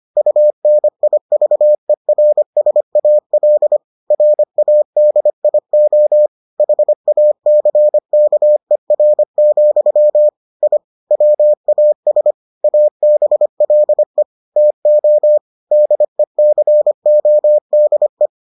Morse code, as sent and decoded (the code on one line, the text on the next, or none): ..- -. .. ...- . .-. ... .- .-.. / .-. .- -.. .. --- / .... .- -.-. -.- . .-. --..-- / .. / .-- .- ... / .- -... .-.. / - --- / -.. -.-. --- -.. .
UNIVERSAL RADIO HACKER, I WAS ABLE TO DECODE